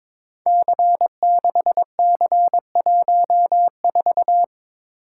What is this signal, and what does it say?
Morse code, heard as a keyed tone, about 22 wpm, 700 Hz. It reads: C6C14
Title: Code Groups - 5 Letter:Numbers - 22wpm